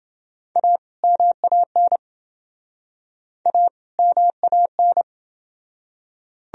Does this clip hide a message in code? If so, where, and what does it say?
Yes: Morse code from 0.6 to 5.0 s, heard as a keyed tone, about 28 wpm, 700 Hz. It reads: A MAN A MAN